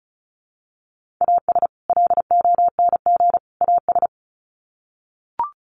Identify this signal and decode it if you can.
Morse code, heard as a keyed tone, about 34 wpm, 700 Hz. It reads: AS LONG AS
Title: Sets of 3 Words 35wpm